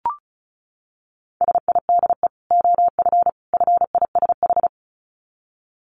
Morse code, heard as a keyed tone, about 34 wpm, 700 Hz. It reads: SIDE OF FISH
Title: Dit Challenge 35wpm